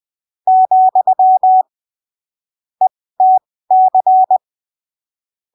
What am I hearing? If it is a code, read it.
Morse code, heard as a keyed tone, about 20 wpm, spaced as for 15 wpm, 750 Hz. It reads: , ETC
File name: arquivos / texto / texto 18 20ppm